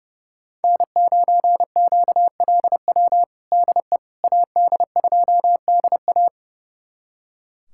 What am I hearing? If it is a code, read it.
Morse code, heard as a keyed tone, about 30 wpm, 700 Hz. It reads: N9QLW DE AD2DA
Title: USA Callsign DE Callsign - 5 Characters - 30wpm